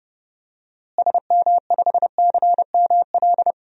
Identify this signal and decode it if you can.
Morse code, heard as a keyed tone, about 30 wpm, 700 Hz. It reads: SM5CML
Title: DXCC-100 Call Signs - 6 Characters - Repeated 3x - 30wpm